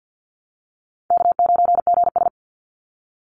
Morse code, heard as a keyed tone, about 50 wpm, 700 Hz. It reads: K9GS
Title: Call Signs - 50wpm